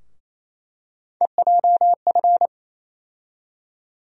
Morse code, heard as a keyed tone, about 28 wpm, 700 Hz. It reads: EJF
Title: USA Call Sign Postfixes - 3 Letters 28wpm